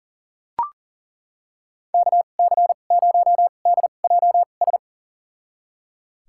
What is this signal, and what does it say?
Morse code, heard as a keyed tone, about 39 wpm, spaced as for 28 wpm, 700 Hz. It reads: KC0DJS
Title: USA Call Signs - 6 Characters - 2x Character Spacing - 40wpm